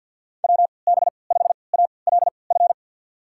Morse code, heard as a keyed tone, about 53 wpm, spaced as for 29 wpm, 700 Hz. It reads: WB5ALF
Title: USA Call Signs - 6 Characters - 3x Character Spacing - 50wpm